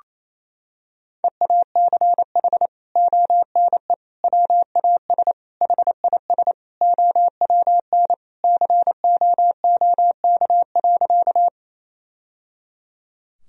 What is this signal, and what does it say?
Morse code, heard as a keyed tone, about 28 wpm, 700 Hz. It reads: EACH ONE WAS HIS OWN COOK.
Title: Sentences from Top 1000 Words - Review Part 4 - 28wpm